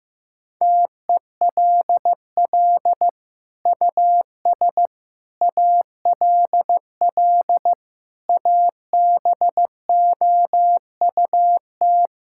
Morse code, heard as a keyed tone, about 15 wpm, 700 Hz. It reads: TELL US ALL ABOUT
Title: Sentences from Top 1000 Words - Review Part 5 - 15wpm